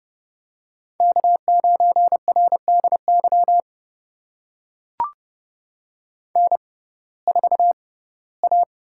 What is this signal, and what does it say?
Morse code, heard as a keyed tone, about 32 wpm, spaced as for 19 wpm, 700 Hz. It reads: K9RDY N4A
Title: USA Call Signs - 5 Characters - 6x Character Spacing - 30wpm